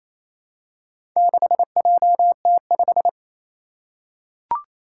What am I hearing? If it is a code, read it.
Morse code, heard as a keyed tone, about 27 wpm, 700 Hz. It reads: 6JT5
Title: Four Challenging Letter-Number Combinations 28wpm